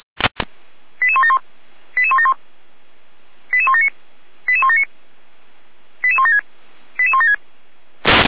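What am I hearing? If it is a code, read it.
Five-tone selective call (ZVEI), heard as quick runs of five tones, tones of about 70 ms. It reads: 88171 88171 88179 88179 88176 88176
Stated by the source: ZVEI five-tone selective call received by SDR.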